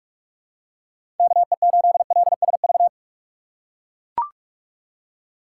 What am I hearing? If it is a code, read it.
Morse code, heard as a keyed tone, about 43 wpm, 700 Hz. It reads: KE8LSV
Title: USA Call Signs - 6 Characters - 4x Character Spacing - 45wpm